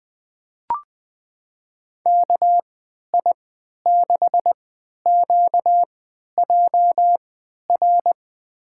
Morse code, heard as a keyed tone, about 20 wpm, spaced as for 11 wpm, 700 Hz. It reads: KI6QJR
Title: USA Call Signs - 3x Character Spacing - 20wpm